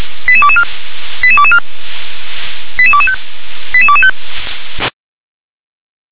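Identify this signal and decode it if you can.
Five-tone selective call (ZVEI), heard as quick runs of five tones, tones of about 70 ms. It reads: 88224 88224 88225 88225
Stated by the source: ZVEI five-tone selective call 88221 received by SDR.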